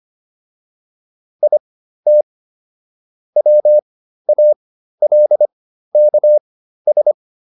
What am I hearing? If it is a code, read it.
Morse code, heard as a keyed tone, about 25 wpm, spaced as for 13 wpm, 600 Hz. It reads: IT WALKS